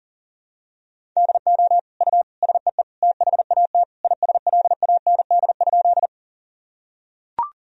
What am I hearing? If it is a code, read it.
Morse code, heard as a keyed tone, about 40 wpm, 700 Hz. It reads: DO U SEE THAT ISLAND?
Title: Sentences from Top 500 Words and Abbreviations 40wpm